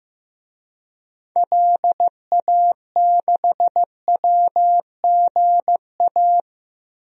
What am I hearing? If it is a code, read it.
Morse code, heard as a keyed tone, about 15 wpm, 700 Hz. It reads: LA6WGA
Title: DXCC-100 Call Signs - 6 Characters - Repeated 2x - 15wpm